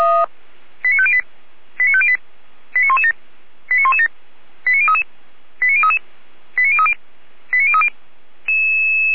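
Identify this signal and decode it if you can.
Five-tone selective call (ZVEI), heard as quick runs of five tones, tones of about 80 ms. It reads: 89508 89508 89117 89117 89033 89033 89030 89030
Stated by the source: ZVEI five-tone call received by SDR.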